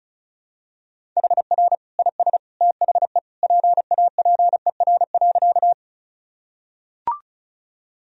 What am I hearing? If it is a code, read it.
Morse code, heard as a keyed tone, about 34 wpm, 700 Hz. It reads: HR IS THE PAPER.
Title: Sentences from Top 300 Words and Abbreviations 35wpm